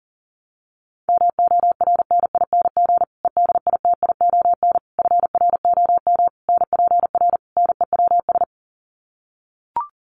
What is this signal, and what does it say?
Morse code, heard as a keyed tone, about 40 wpm, 700 Hz. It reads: MORNING EDITION FROM NPR NEWS
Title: Sets of 5 Words Encore 40wpm